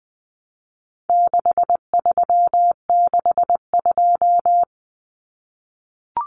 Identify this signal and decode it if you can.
Morse code, heard as a keyed tone, about 20 wpm, 700 Hz. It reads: 6362
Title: Numbers - 4-Digit 20wpm